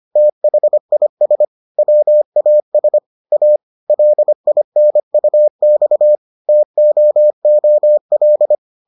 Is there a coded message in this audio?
Yes: Morse code from 0.2 to 8.5 s, heard as a keyed tone, about 25 wpm, 600 Hz. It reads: THIS WAS A LINUX TOOL